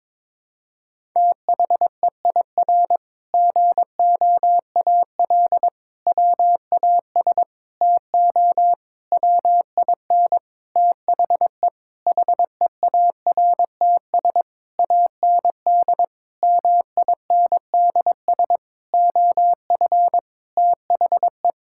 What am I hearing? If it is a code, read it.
Morse code, heard as a keyed tone, about 22 wpm, 700 Hz. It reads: THEIR GOAL WAS TO WIN THE HEARTS AND MINDS OF THE
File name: practice / 22wpm / Sentences with 8-Word Phrases - 22wpm